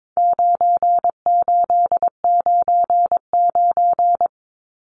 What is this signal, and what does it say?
Morse code, heard as a keyed tone, about 22 wpm, 700 Hz. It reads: 9899
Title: Numbers - 4-Digit 22wpm